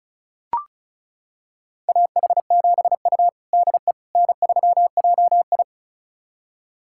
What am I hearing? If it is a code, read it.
Morse code, heard as a keyed tone, about 36 wpm, 690 Hz. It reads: AH7U DE N3JI
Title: USA Callsign DE Callsign - 4 Characters - 35wpm